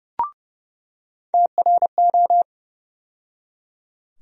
Morse code, heard as a keyed tone, about 30 wpm, 700 Hz. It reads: TRO
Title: USA Call Sign Postfixes - 3 Letters 30wpm